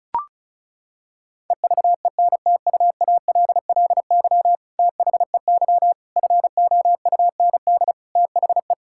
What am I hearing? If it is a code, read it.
Morse code, heard as a keyed tone, about 36 wpm, 700 Hz. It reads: EVENTUALLY THEY FOUND THE
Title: Sentences from Top 1000 Words - Review Part 6 - 35wpm